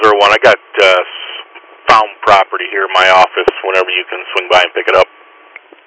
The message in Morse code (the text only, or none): none